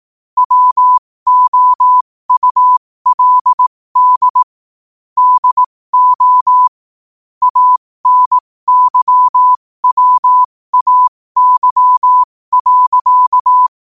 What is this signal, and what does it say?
Morse code, heard as a keyed tone, about 18 wpm, spaced as for 15 wpm, 1000 Hz. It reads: WOULD DO ANYWAY.
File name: qotdfiles / 15wpm